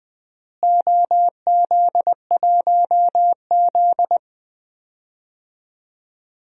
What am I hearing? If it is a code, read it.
Morse code, heard as a keyed tone, about 20 wpm, 700 Hz. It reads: OZ1Z
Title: DXCC-100 Call Signs - 4 Characters - Repeated 2x - 20wpm